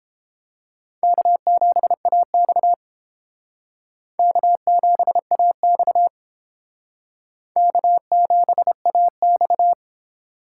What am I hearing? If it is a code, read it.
Morse code, heard as a keyed tone, about 31 wpm, 700 Hz. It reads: K7AX K7AX K7AX